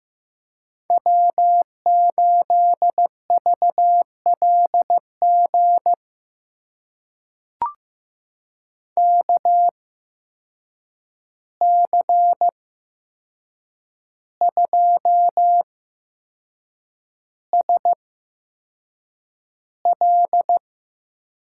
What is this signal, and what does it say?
Morse code, heard as a keyed tone, about 15 wpm, spaced as for 6 wpm, 700 Hz. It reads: W8VLG KC2SL
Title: USA Call Signs - 8x Character Spacing - 15wpm